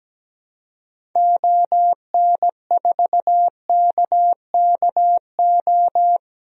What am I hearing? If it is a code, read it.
Morse code, heard as a keyed tone, about 17 wpm, 700 Hz. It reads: ON4KKO
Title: DXCC-100 Call Signs - 6 Characters - 17wpm